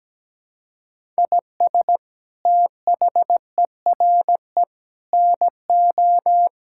Morse code, heard as a keyed tone, about 17 wpm, 700 Hz. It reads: IS THERE NO
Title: Sentences from Top 1000 Words - Review Part 5 - 17wpm